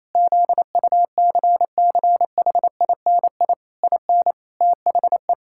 Morse code, heard as a keyed tone, about 28 wpm, 700 Hz. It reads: ZUCCHINI IN THE
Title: Pangram Challenge v2 - 28wpm